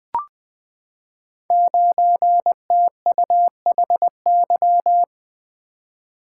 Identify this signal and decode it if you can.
Morse code, heard as a keyed tone, about 20 wpm, 700 Hz. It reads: 9TUHY
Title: Code Groups - 5 Letter:Numbers - 20wpm